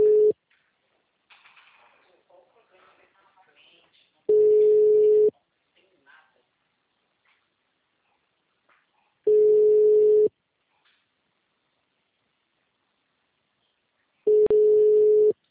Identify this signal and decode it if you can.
Ringback tone (Europe, 425 Hz), with speech.